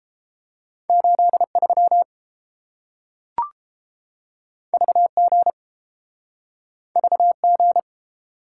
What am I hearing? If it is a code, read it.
Morse code, heard as a keyed tone, about 30 wpm, 700 Hz. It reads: 83 VG VG